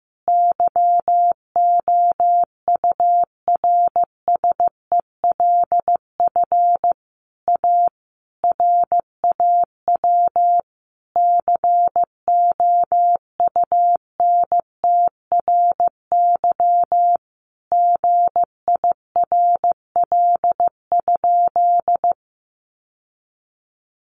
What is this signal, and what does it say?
Morse code, heard as a keyed tone, about 15 wpm, 700 Hz. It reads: YOURSELF A RAW COUNTRY GIRL?
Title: Sentences from Top 1500 Words 15wpm